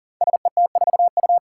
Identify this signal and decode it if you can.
Morse code, heard as a keyed tone, about 38 wpm, 700 Hz. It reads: SET4U
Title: Five Challenging Letter-Number Combinations 40wpm